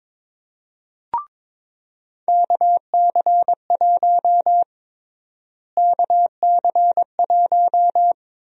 Morse code, heard as a keyed tone, about 22 wpm, 700 Hz. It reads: KC1 KC1
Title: USA Call Sign Prefixes - Repeated 2x - 22wpm